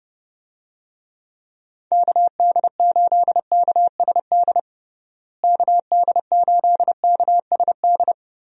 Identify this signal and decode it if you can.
Morse code, heard as a keyed tone, about 30 wpm, 700 Hz. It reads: KD8KSD KD8KSD
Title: USA Call Signs - 6 Characters - Repeated 2x - 30wpm